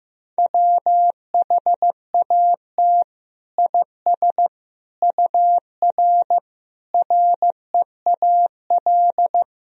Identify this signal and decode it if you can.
Morse code, heard as a keyed tone, about 16 wpm, 700 Hz. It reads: WHAT IS UR REAL
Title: Sentences from Top 300 Words and Abbreviations 15wpm